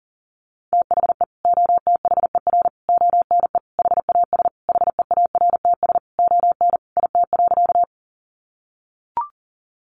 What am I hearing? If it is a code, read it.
Morse code, heard as a keyed tone, about 40 wpm, 700 Hz. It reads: THE OTHER ONE HAS HEARTS ON IT.
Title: Additional Easy Phrases 40wpm